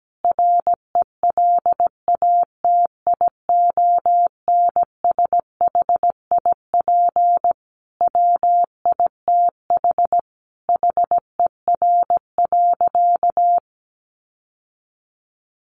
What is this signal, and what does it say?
Morse code, heard as a keyed tone, about 17 wpm, 700 Hz. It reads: RELATIONSHIP WITH HER.
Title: Sentences from Top 1100 Words 17wpm